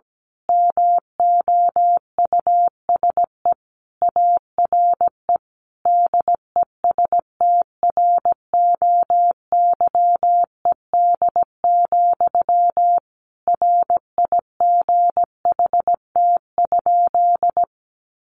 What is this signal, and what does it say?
Morse code, heard as a keyed tone, about 17 wpm, 700 Hz. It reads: MOUSE ARE DESTROYED, RIGHT?
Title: A Sound Of Thunder 17wpm